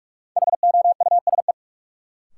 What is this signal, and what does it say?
Morse code, heard as a keyed tone, about 48 wpm, 700 Hz. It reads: HOUSE
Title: Sets of 2 Words - New Word Drill - 45wpm